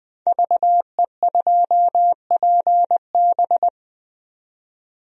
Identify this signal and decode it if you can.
Morse code, heard as a keyed tone, about 21 wpm, 700 Hz. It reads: VE2PB
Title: DXCC-100 Call Signs - 5 Characters - 20wpm